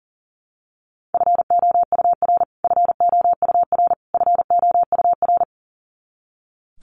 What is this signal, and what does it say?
Morse code, heard as a keyed tone, about 40 wpm, 700 Hz. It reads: FOUR FOUR FOUR
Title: Top 500 Words Repeated 40wpm